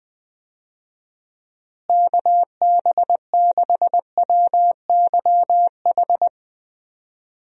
Morse code, heard as a keyed tone, about 20 wpm, 700 Hz. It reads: KB6WYH
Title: USA Call Signs - 6 Characters - Repeated 3x - 20wpm